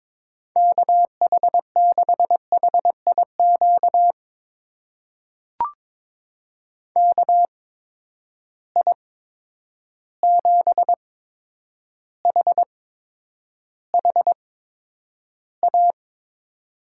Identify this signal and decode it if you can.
Morse code, heard as a keyed tone, about 21 wpm, spaced as for 9 wpm, 700 Hz. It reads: KH6HIQ KI7HHA
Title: USA Call Signs - 6 Characters - 8x Character Spacing - 22wpm